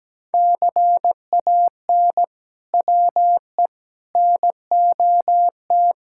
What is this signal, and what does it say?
Morse code, heard as a keyed tone, about 17 wpm, 700 Hz. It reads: CAN WE NOT
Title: Sentences from Top 1000 Words - Review Part 6 - 17wpm